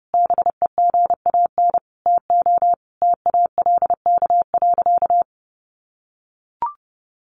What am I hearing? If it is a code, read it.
Morse code, heard as a keyed tone, about 30 wpm, 700 Hz. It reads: BEGAN TO TALK.
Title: Sentences from Top 400 Words 30wpm